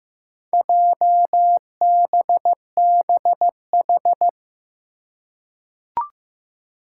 Morse code, heard as a keyed tone, about 15 wpm, 700 Hz. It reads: JBBH
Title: Four Challenging Letter-Number Combinations 15wpm